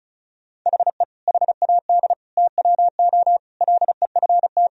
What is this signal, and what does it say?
Morse code, heard as a keyed tone, about 36 wpm, 700 Hz. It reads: HE HAD TWO LEFT
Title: Sentences from Top 1000 Words - Review Part 1 - 35wpm